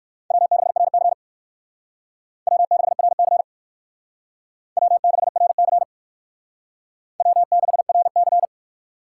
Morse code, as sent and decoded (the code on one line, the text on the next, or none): .-- -.... .-. -.-. / .-- -.... .-. -.-. / .-- -.... .-. -.-. / .-- -.... .-. -.-.
W6RC W6RC W6RC W6RC